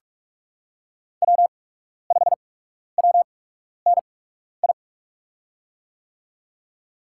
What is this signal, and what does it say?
Morse code, heard as a keyed tone, about 44 wpm, spaced as for 12 wpm, 690 Hz. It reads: W5WNI
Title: USA Call Signs - 5 Characters - 8x Character Spacing - 45wpm